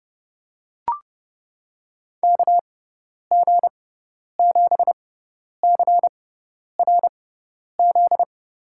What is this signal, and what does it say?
Morse code, heard as a keyed tone, about 30 wpm, spaced as for 10 wpm, 700 Hz. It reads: KG7CRZ
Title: USA Call Signs - 6x Character Spacing - 30wpm